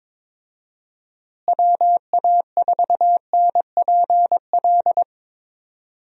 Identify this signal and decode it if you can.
Morse code, heard as a keyed tone, about 22 wpm, 700 Hz. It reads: WA4NPL
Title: USA Call Signs - 6 Characters - Repeated 3x - 22wpm